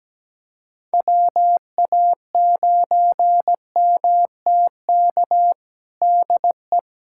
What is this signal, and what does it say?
Morse code, heard as a keyed tone, about 17 wpm, 700 Hz. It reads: WA9MTK DE
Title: USA Callsign DE Callsign - 6 Characters - 17wpm